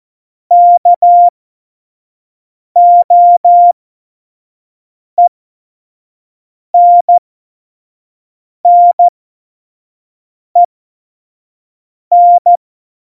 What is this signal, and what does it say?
Morse code, heard as a keyed tone, about 14 wpm, spaced as for 5 wpm, 700 Hz. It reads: KOENNEN
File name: Morsekurs / Witze / Witz 26 Mai 2020, 25 F70